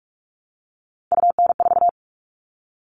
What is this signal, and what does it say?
Morse code, heard as a keyed tone, about 44 wpm, 700 Hz. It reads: UN4
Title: All Cut Numbers - 3-Digit 45wpm